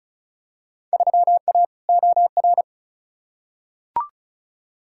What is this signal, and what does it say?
Morse code, heard as a keyed tone, about 34 wpm, 700 Hz. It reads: 3A OR
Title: Field Day Exchanges Encore - 35wpm